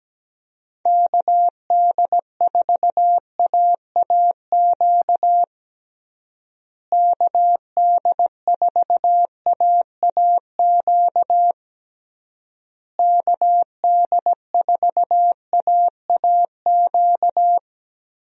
Morse code, heard as a keyed tone, about 17 wpm, 700 Hz. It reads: KD4AAQ KD4AAQ KD4AAQ
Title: USA Call Signs - 6 Characters - Repeated 3x - 17wpm